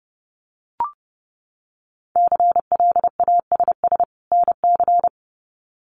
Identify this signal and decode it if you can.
Morse code, heard as a keyed tone, about 30 wpm, 700 Hz. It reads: CLASS NC
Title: Winter Field Day Exchanges 30wpm